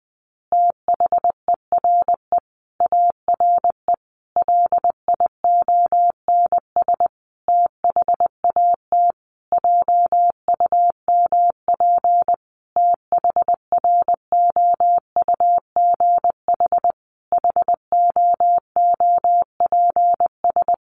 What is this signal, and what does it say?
Morse code, heard as a keyed tone, about 20 wpm, 700 Hz. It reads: THERE ARE LIONS THAT JUMP THROUGH HOOPS
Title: Easy Phrases 20wpm